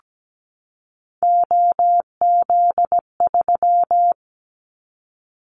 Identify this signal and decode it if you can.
Morse code, heard as a keyed tone, about 17 wpm, 700 Hz. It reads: OZ3
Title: Callsign Prefixes for Top 50 DXCC - 17wpm